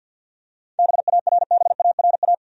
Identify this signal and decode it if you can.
Morse code, heard as a keyed tone, about 47 wpm, 700 Hz. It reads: BARBARA
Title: USA Common Names - 50wpm